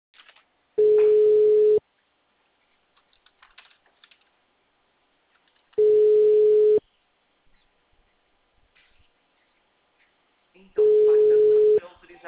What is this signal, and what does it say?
Ringback tone (Europe, 425 Hz), with speech.